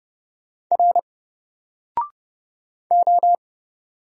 Morse code, heard as a keyed tone, about 29 wpm, 700 Hz. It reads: R O
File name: Single-Letter-Number-30wpm